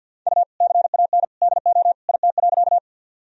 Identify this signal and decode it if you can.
Morse code, heard as a keyed tone, about 52 wpm, 700 Hz. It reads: U CAN DO IT.
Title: Sentences from Top 100 Words and Abbreviations 50wpm